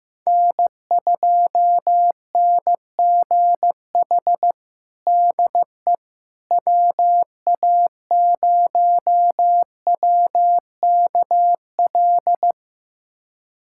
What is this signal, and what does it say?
Morse code, heard as a keyed tone, about 15 wpm, 700 Hz. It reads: N2NGH DE WA0WKL
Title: USA Callsign DE Callsign 15wpm